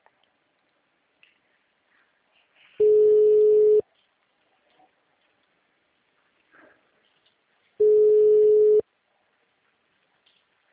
Ringback tone (Europe, 425 Hz).
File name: out- -2003- - - .14